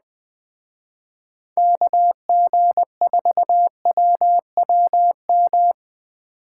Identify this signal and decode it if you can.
Morse code, heard as a keyed tone, about 20 wpm, 700 Hz. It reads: KG4WWM
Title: Speed Racing - USA 2x3 Call Signs 17wpm